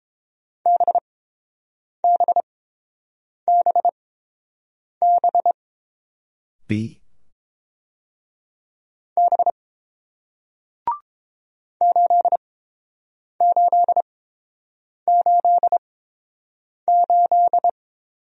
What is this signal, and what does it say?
Morse code, heard as a keyed tone, about 30 wpm, spaced as for 7 wpm, 700 Hz. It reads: BBBB B 8888